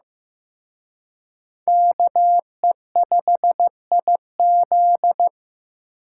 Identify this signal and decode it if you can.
Morse code, heard as a keyed tone, about 18 wpm, 700 Hz. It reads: KE5IZ
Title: Speed Racing - USA 2x2 Call Signs 15wpm